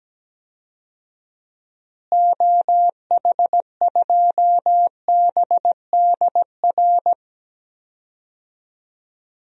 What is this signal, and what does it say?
Morse code, heard as a keyed tone, about 17 wpm, 700 Hz. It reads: OH2BDR
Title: DXCC-100 Call Signs - 6 Characters - Repeated 2x - 17wpm